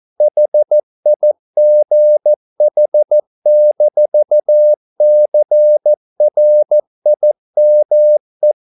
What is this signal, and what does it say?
Morse code, heard as a keyed tone, about 14 wpm, 600 Hz. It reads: HIGH-CRIME